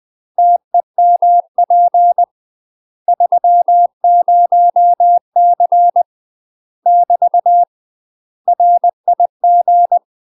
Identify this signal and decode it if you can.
Morse code, heard as a keyed tone, about 20 wpm, 700 Hz. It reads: TEMP 30C <BT> RIG
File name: UK MASTER QSOs TRAINING / 20WPM / UK MASTER QSOs 0066